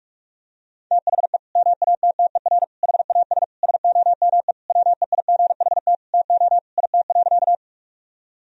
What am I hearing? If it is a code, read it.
Morse code, heard as a keyed tone, about 42 wpm, 700 Hz. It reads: THE MATTER HAS SOME WEIGHT TO IT.
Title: Sentences from Top 1000 Words - Review Part 3 - 45wpm